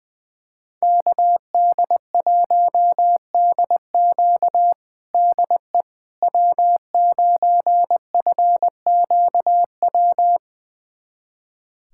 Morse code, heard as a keyed tone, about 21 wpm, 700 Hz. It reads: KD1DQ DE W9FQW
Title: USA Callsign DE Callsign - 5 Characters - 20wpm